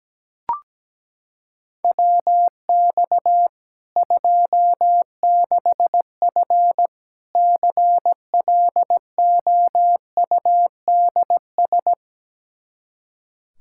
Morse code, heard as a keyed tone, about 17 wpm, 700 Hz. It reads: WX 26F CLOUDS
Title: Practice QSO - Exchange 2 - Info - 17wpm